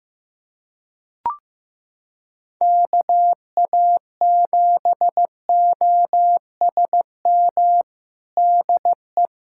Morse code, heard as a keyed tone, about 16 wpm, 700 Hz. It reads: KA7OSM DE
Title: USA Callsign DE Callsign 15wpm